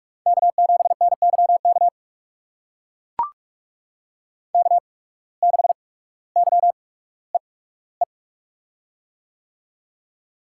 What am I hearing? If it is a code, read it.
Morse code, heard as a keyed tone, about 44 wpm, spaced as for 19 wpm, 700 Hz. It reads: K7NYK K6YEE
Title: USA Call Signs - 5 Characters - 8x Character Spacing - 45wpm